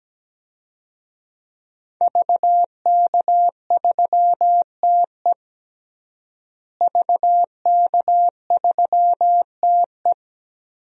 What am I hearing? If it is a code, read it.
Morse code, heard as a keyed tone, about 17 wpm, 700 Hz. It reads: VK3TE VK3TE
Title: DXCC-100 Call Signs - 5 Characters - Repeated 3x - 17wpm